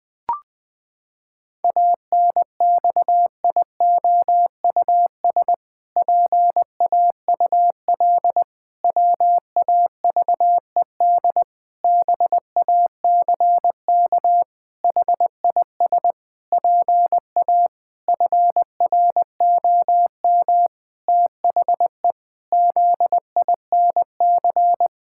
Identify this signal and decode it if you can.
Morse code, heard as a keyed tone, about 20 wpm, 700 Hz. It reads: ANXIOUS PAUL WAVED BACK HIS PA FROM THE ZINC
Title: Pangram Challenge v2 - 20wpm